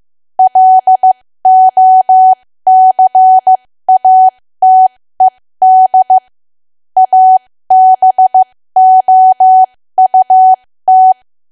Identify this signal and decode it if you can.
Morse code, heard as a keyed tone, about 15 wpm, spaced as for 13 wpm, 750 Hz. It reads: LOCATED ABOUT
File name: cwa / W1AW-Level2 / 13WPM